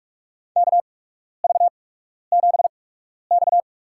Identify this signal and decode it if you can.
Morse code, heard as a keyed tone, about 43 wpm, spaced as for 12 wpm, 700 Hz. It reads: KV7X
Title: USA Call Signs - 4 Characters - 8x Character Spacing - 45wpm